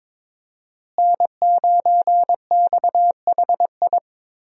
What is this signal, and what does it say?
Morse code, heard as a keyed tone, about 22 wpm, 700 Hz. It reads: N9XHI
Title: USA Call Signs - 5 Characters - Repeated 3x - 22wpm